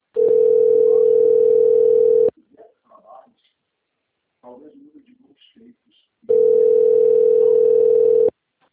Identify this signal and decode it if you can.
Ringback tone (North America), 440 + 480 Hz, with speech.
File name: rg-1600-unknown- - - .28180